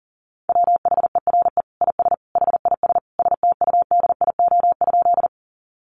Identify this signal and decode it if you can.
Morse code, heard as a keyed tone, about 40 wpm, 700 Hz. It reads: WHERE IS HIS STUDIO?
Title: Sentences from Top 1200 Words 40wpm